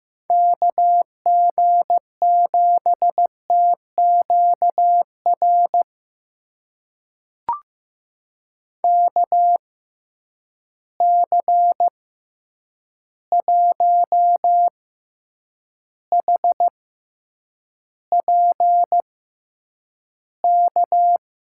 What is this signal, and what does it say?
Morse code, heard as a keyed tone, about 15 wpm, spaced as for 8 wpm, 700 Hz. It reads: KG7TQR KC1HPK
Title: USA Call Signs - 6 Characters - 6x Character Spacing - 15wpm